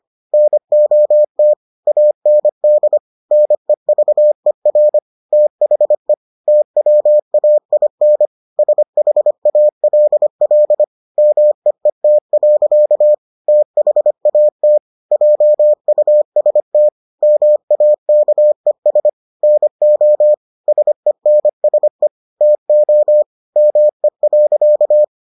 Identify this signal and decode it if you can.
Morse code, heard as a keyed tone, about 25 wpm, 600 Hz. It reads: NOT AND NEVER THE TWAIN SHALL MEET. THAT JUST MAKES NO SENSE TO ME.